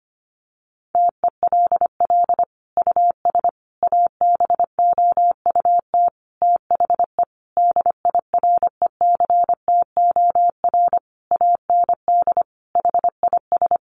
Morse code, heard as a keyed tone, about 25 wpm, 700 Hz. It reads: TELL US ABOUT THE DIRECTOR AND HIS
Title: Sentences from Top 1300 Words 25wpm